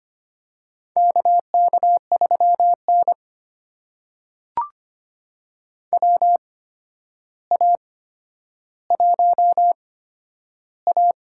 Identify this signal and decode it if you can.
Morse code, heard as a keyed tone, about 25 wpm, spaced as for 11 wpm, 700 Hz. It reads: KK3N WA1A
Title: USA Call Signs - 4 Characters - 8x Character Spacing - 25wpm